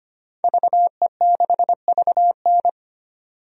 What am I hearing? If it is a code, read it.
Morse code, heard as a keyed tone, about 25 wpm, 700 Hz. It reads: VE6VN
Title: DXCC-100 Call Signs - 5 Characters - 25wpm